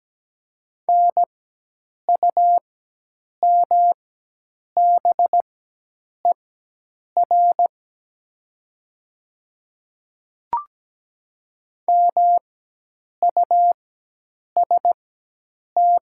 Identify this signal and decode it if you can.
Morse code, heard as a keyed tone, about 17 wpm, spaced as for 8 wpm, 700 Hz. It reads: NUMBER MUST
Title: 4x Character Spacing - Top 300 Words - 17wpm